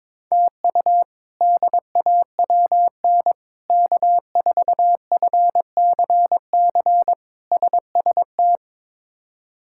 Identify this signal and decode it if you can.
Morse code, heard as a keyed tone, about 22 wpm, 700 Hz. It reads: TU DAWN K4FCC SST
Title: K1USN SST QSO Practice - 22wpm